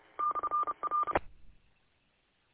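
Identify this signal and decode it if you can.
Morse code, heard as a keyed tone, about 30 wpm, 1200 Hz. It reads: /R
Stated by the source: Audio from the W8UM 145.230 repeater archive.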